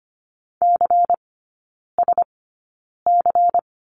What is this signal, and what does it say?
Morse code, heard as a keyed tone, about 25 wpm, spaced as for 9 wpm, 700 Hz. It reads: CSC
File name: course / 25:10wpm / 093 Lesson 16e - Letter c - Three Characters 25wpm